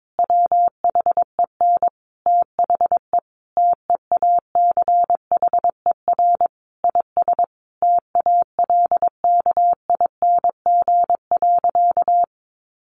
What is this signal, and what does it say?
Morse code, heard as a keyed tone, about 22 wpm, 700 Hz. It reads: WHEN THE TEACHER IS TALKING.
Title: Additional Easy Phrases 22wpm